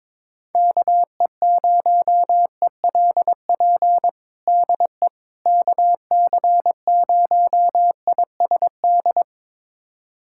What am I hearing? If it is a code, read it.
Morse code, heard as a keyed tone, about 22 wpm, 700 Hz. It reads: KE0ELP DE KC0ISD
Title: USA Callsign DE Callsign - 6 Characters - 22wpm